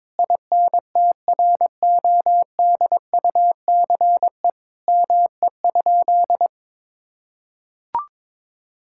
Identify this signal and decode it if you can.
Morse code, heard as a keyed tone, about 22 wpm, 700 Hz. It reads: INTRODUCE ME?
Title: Sentences from Top 1600 Words 22wpm